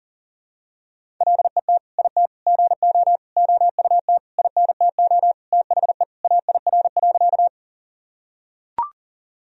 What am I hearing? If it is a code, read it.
Morse code, heard as a keyed tone, about 40 wpm, 700 Hz. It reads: LET IT GO OUT INTO THE AIR.
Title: Sentences from Top 1000 Words - Review Part 1 - 40wpm